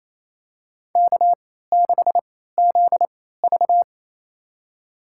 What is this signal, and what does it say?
Morse code, heard as a keyed tone, about 28 wpm, spaced as for 16 wpm, 700 Hz. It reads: K6ZV
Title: USA Call Signs - 4 Characters - 3x Character Spacing - 28wpm